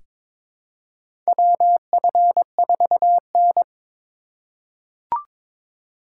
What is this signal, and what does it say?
Morse code, heard as a keyed tone, about 22 wpm, 700 Hz. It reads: WF4N
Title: USA Call Signs - 4 Characters - 4x Character Spacing - 22wpm